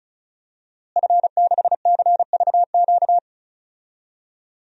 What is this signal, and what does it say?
Morse code, heard as a keyed tone, about 36 wpm, 700 Hz. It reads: F6CVQ
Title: DXCC-100 Call Signs - 5 Characters - Repeated 3x - 35wpm